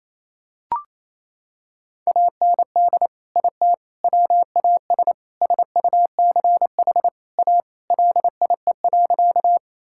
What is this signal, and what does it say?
Morse code, heard as a keyed tone, about 28 wpm, 700 Hz. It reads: AND IT WAS SUCH A LIE.
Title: Sentences from Top 1600 Words 28wpm